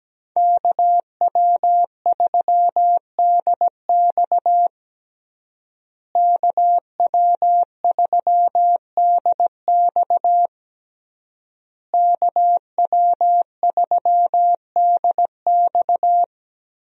Morse code, heard as a keyed tone, about 17 wpm, 700 Hz. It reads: KW3DX KW3DX KW3DX
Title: USA Call Signs - 5 Characters - Repeated 3x - 17wpm